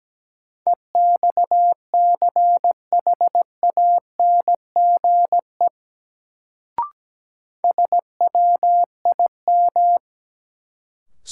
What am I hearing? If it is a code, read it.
Morse code, heard as a keyed tone, about 17 wpm, 700 Hz. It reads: EXCHANGE SWIM
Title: Sentences from Top 1000 Words - Review Part 6 - New Word Drill - 17wpm